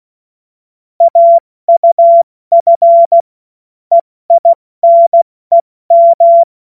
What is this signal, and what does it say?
Morse code, heard as a keyed tone, about 16 wpm, 680 Hz. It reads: AUF EINEM
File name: Witz-21-Oktober-2025-70-F80